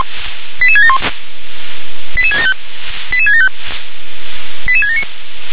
Five-tone selective call (ZVEI), heard as quick runs of five tones, tones of about 70 ms. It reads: 88671 88675 88675 88679